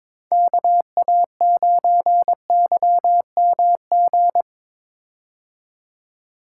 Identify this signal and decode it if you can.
Morse code, heard as a keyed tone, about 22 wpm, 700 Hz. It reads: KA9YMG
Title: K1USN SST QSO Practice - 22wpm